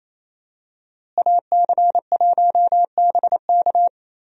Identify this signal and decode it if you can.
Morse code, heard as a keyed tone, about 27 wpm, 700 Hz. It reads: AC1BK
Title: USA Call Signs - 5 Characters - Repeated 3x - 28wpm